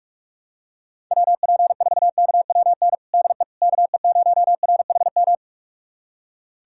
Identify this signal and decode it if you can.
Morse code, heard as a keyed tone, about 46 wpm, 700 Hz. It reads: WP4KWN DE KE0RHM
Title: USA Callsign DE Callsign - 6 Characters - 45wpm